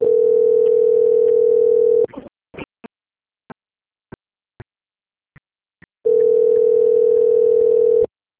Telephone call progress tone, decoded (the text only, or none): ringback tone